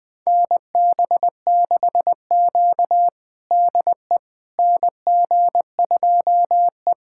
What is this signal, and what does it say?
Morse code, heard as a keyed tone, about 20 wpm, 700 Hz. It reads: NB6Q DE NG2E
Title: USA Callsign DE Callsign - 4 Characters - 20wpm